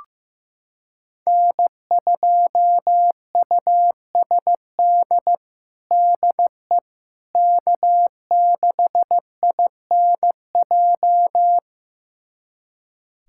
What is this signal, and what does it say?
Morse code, heard as a keyed tone, about 15 wpm, 700 Hz. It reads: N2USD DE K6INJ
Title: USA Callsign DE Callsign - 5 Characters - 15wpm